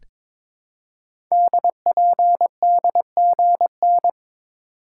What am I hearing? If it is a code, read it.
Morse code, heard as a keyed tone, about 22 wpm, 700 Hz. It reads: DPDGN
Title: Code Groups - 5 Letters - 22wpm